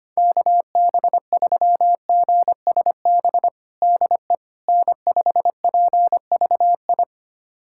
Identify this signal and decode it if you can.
Morse code, heard as a keyed tone, about 25 wpm, 700 Hz. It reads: KB3GSB DE N5PVI
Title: USA Callsign DE Callsign 25wpm